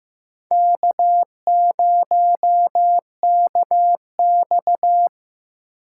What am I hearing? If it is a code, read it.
Morse code, heard as a keyed tone, about 15 wpm, 700 Hz. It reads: K0KX
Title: K1USN SST QSO Practice - 15wpm